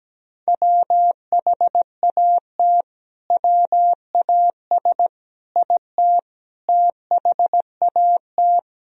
Morse code, heard as a keyed tone, about 17 wpm, 700 Hz. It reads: WHAT WAS IT THAT
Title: Sentences from Top 300 Words and Abbreviations 17wpm